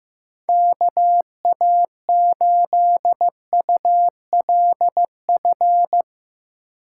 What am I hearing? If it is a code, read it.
Morse code, heard as a keyed tone, about 15 wpm, 700 Hz. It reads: KA8ULF
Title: USA Call Signs - 6 Characters - 6x Character Spacing - 15wpm